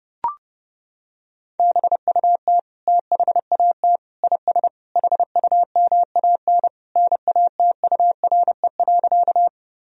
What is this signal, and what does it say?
Morse code, heard as a keyed tone, about 32 wpm, 700 Hz. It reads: BUT THAT IS HUMAN NATURE.
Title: Sentences from Top 1000 Words - Review Part 3 - 30wpm